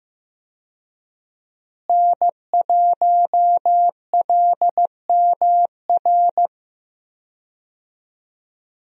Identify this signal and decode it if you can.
Morse code, heard as a keyed tone, about 15 wpm, 700 Hz. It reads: N1LMR
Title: USA Call Signs - 5 Characters - Repeated 2x - 15wpm